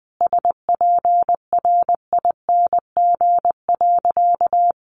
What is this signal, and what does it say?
Morse code, heard as a keyed tone, about 20 wpm, 700 Hz. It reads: SPRING.
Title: Sentences from Top 1200 Words 20wpm